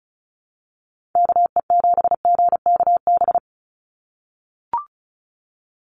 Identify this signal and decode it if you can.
Morse code, heard as a keyed tone, about 36 wpm, 700 Hz. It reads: KE7GKB
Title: USA Call Signs 35wpm